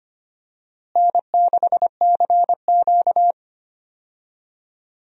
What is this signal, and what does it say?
Morse code, heard as a keyed tone, about 25 wpm, 700 Hz. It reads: N6CQ
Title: USA Call Signs - 4 Characters - Repeated 2x - 25wpm